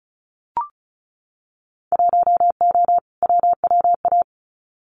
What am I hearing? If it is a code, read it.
Morse code, heard as a keyed tone, about 36 wpm, 690 Hz. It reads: 1O WWA
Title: Winter Field Day Exchanges 35wpm